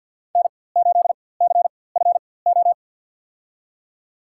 Morse code, heard as a keyed tone, about 47 wpm, spaced as for 23 wpm, 700 Hz. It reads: N8CFO
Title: USA Call Signs - 5 Characters - 4x Character Spacing - 50wpm